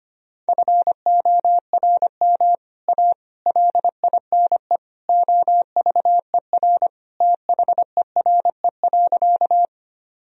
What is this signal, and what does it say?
Morse code, heard as a keyed tone, about 25 wpm, 700 Hz. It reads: FORM A LINE OVER THERE.
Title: Sentences from Top 1000 Words - Review Part 1 - 25wpm